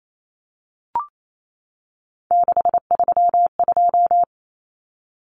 Morse code, heard as a keyed tone, about 28 wpm, 700 Hz. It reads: 632
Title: Numbers - 3-Digit 28wpm